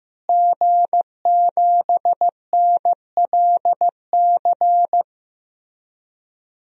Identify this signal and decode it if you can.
Morse code, heard as a keyed tone, about 15 wpm, 700 Hz. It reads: G7NLC
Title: DXCC-100 Call Signs - 5 Characters - Repeated 2x - 15wpm